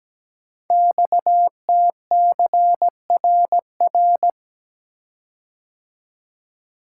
Morse code, heard as a keyed tone, about 17 wpm, 700 Hz. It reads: XTCRR
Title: Code Groups - 5 Letters - 17wpm